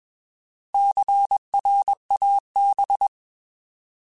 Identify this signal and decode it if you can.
Morse code, heard as a keyed tone, about 21 wpm, 800 Hz. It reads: CRAB